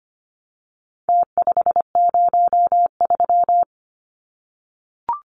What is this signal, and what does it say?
Morse code, heard as a keyed tone, about 25 wpm, 700 Hz. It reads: T503
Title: Numbers - Contest Style - 4-Digit 25wpm